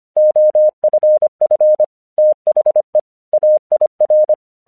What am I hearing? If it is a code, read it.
Morse code, heard as a keyed tone, about 26 wpm, 610 Hz. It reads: OFF THE AIR
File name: 1 English 25wpm / AQC13